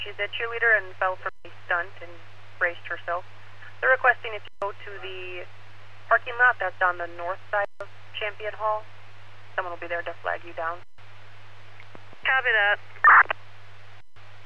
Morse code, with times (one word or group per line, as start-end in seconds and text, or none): none